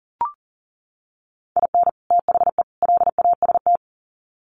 Morse code, heard as a keyed tone, about 40 wpm, 700 Hz. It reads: IN THE LAST
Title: Sets of 3 Words 40wpm